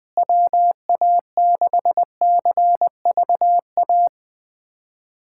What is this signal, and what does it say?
Morse code, heard as a keyed tone, about 21 wpm, 700 Hz. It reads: WA6CVA
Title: USA Callsign DE Callsign - 6 Characters - 20wpm